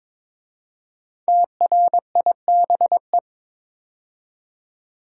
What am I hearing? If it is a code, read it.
Morse code, heard as a keyed tone, about 22 wpm, 700 Hz. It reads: TRIBE
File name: 5 Letter Words - First 3 Letters Given - 22wpm